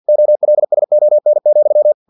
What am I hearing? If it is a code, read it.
Morse code, heard as a keyed tone, about 50 wpm, 600 Hz. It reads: OLSON,
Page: Never 49wpm